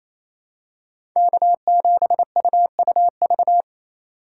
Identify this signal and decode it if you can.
Morse code, heard as a keyed tone, about 28 wpm, 700 Hz. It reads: K7UUV
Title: USA Call Signs - 5 Characters - 2x Character Spacing - 28wpm